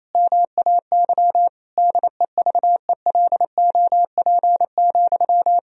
Morse code, heard as a keyed tone, about 28 wpm, 700 Hz. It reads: MAY DEVELOP,
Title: Sentences from Top 1000 Words - Review Part 2 - 28wpm